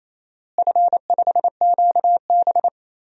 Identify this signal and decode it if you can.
Morse code, heard as a keyed tone, about 28 wpm, 700 Hz. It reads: F5QB
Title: DXCC-100 Call Signs - 4 Characters - Repeated 3x - 28wpm